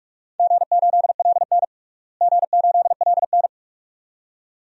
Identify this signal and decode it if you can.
Morse code, heard as a keyed tone, about 46 wpm, 700 Hz. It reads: G8LN G8LN
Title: DXCC-100 Call Signs - 4 Characters - Repeated 2x - 45wpm